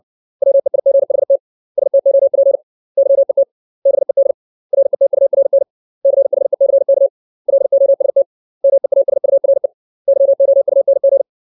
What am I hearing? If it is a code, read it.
Morse code, heard as a keyed tone, about 60 wpm, 550 Hz. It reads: WEIGHT STOP YET BED RETURN CHECK LOST MEASURE YOUNG
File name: 632 words / 632 Words 60WPM